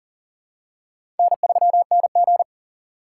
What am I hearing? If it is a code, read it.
Morse code, heard as a keyed tone, about 40 wpm, 700 Hz. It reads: N3NG
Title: USA Call Signs - 4 Characters - 2x Character Spacing - 40wpm